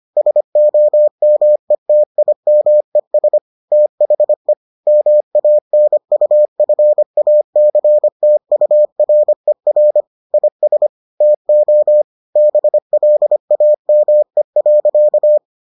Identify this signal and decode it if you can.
Morse code, heard as a keyed tone, about 25 wpm, 600 Hz. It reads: SOMETIMES THE MANUFACTURER IS TO BLAME.